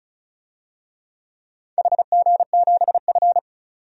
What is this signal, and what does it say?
Morse code, heard as a keyed tone, about 36 wpm, 700 Hz. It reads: HG7F
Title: DXCC-100 Call Signs - 4 Characters - 35wpm